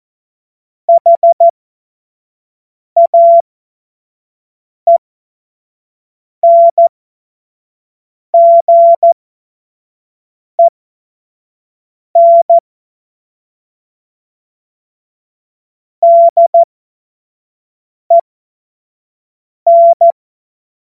Morse code, heard as a keyed tone, about 14 wpm, spaced as for 5 wpm, 680 Hz. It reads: HAENGEN DEN
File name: Morsekurs / Witze / Witz 23. Mai 2023 25 F70